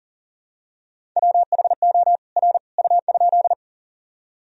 Morse code, heard as a keyed tone, about 37 wpm, 700 Hz. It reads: WHO R U?
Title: Sentences from Top 100 Words and Abbreviations 40wpm